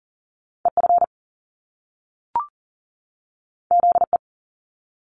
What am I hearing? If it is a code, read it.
Morse code, heard as a keyed tone, about 41 wpm, 700 Hz. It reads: EF ZE
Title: Top Two Letter Combinations in Words 40wpm